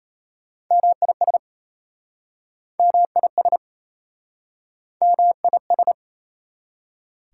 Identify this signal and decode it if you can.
Morse code, heard as a keyed tone, about 38 wpm, 700 Hz. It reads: MIS MIS MIS